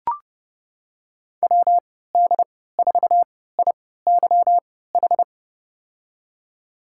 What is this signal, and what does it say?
Morse code, heard as a keyed tone, about 31 wpm, spaced as for 17 wpm, 700 Hz. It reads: WD4IYH
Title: USA Call Signs - 3x Character Spacing - 30wpm